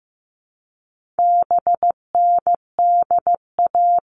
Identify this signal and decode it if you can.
Morse code, heard as a keyed tone, about 15 wpm, 700 Hz. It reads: BNDA
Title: All Cut Numbers - 4-Digit 15wpm